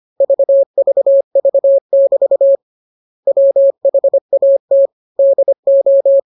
Morse code, heard as a keyed tone, about 25 wpm, 550 Hz. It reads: VVV<BT> WHAT DO
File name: jokes / joke 17 25 0000